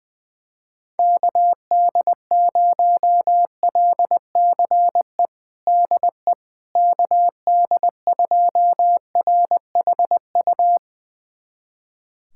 Morse code, heard as a keyed tone, about 20 wpm, 700 Hz. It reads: KD0LCE DE KD2RHU
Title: USA Callsign DE Callsign - 6 Characters - 20wpm